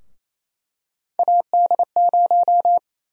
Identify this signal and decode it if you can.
Morse code, heard as a keyed tone, about 28 wpm, 700 Hz. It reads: AD0
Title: USA Call Sign Prefixes - Repeated 2x - 28wpm